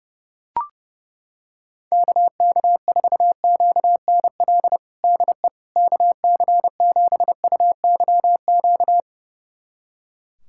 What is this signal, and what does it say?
Morse code, heard as a keyed tone, about 32 wpm, 700 Hz. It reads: KK4QNL DE KC7UYQ
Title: USA Callsign DE Callsign - 6 Characters - 30wpm